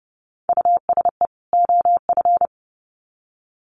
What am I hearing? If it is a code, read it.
Morse code, heard as a keyed tone, about 30 wpm, 700 Hz. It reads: USE OF
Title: Sets of 2 Words 30wpm